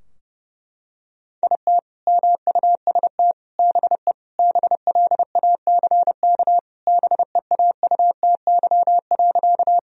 Morse code, heard as a keyed tone, about 30 wpm, 700 Hz. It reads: IT MUST BE BLACK BEAUTY.
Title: Sentences from Top 1000 Words - Review Part 2 - 30wpm